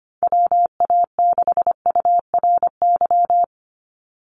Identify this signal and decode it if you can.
Morse code, heard as a keyed tone, about 25 wpm, 700 Hz. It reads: WA6URY
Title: Call Signs - 25wpm